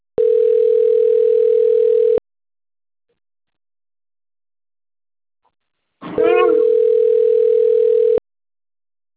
Ringback tone (North America), 440 + 480 Hz.